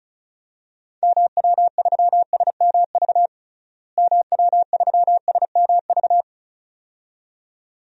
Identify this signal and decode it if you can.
Morse code, heard as a keyed tone, about 34 wpm, 700 Hz. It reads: MW3SMV MW3SMV
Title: DXCC-100 Call Signs - 6 Characters - Repeated 2x - 35wpm